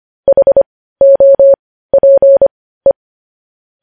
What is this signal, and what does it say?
Morse code, heard as a keyed tone, about 25 wpm, spaced as for 15 wpm, 550 Hz. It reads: HOPE